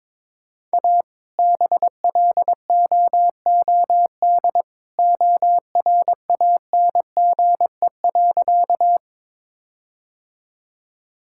Morse code, heard as a keyed tone, about 23 wpm, 700 Hz. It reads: A BLOOD ORANGE.
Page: morse-code-ninja-practice-22wpm